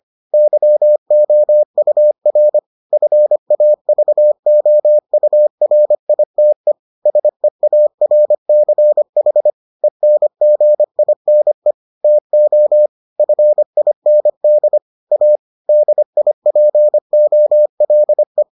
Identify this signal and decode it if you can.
Morse code, heard as a keyed tone, about 25 wpm, 600 Hz. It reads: YOUR FAVOURITE SEARCH ENGINE TO FIND A DIPOLE